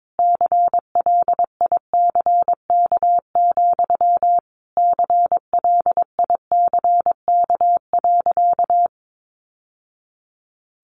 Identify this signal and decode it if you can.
Morse code, heard as a keyed tone, about 22 wpm, 700 Hz. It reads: CLICK, CLICK.
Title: Sentences from Top 1100 Words 22wpm